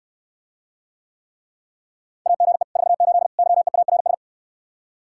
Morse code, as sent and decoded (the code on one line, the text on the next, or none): .- --. / ....- ---.. / -.-- . .- .-. ...
AGE 48 YEARS